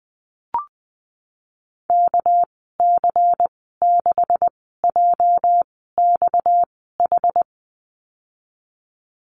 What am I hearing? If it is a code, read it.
Morse code, heard as a keyed tone, about 20 wpm, spaced as for 15 wpm, 700 Hz. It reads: KC6JXH
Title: 231 Lesson 37g - Letter x - Call Signs 20wpm